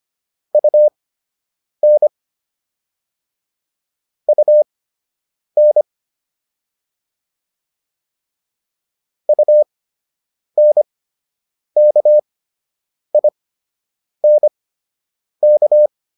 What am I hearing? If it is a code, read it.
Morse code, heard as a keyed tone, about 25 wpm, spaced as for 8 wpm, 600 Hz. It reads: UN UN UNKINK